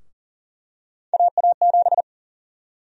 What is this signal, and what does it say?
Morse code, heard as a keyed tone, about 40 wpm, 700 Hz. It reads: AA7
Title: USA Call Sign Prefixes - 40wpm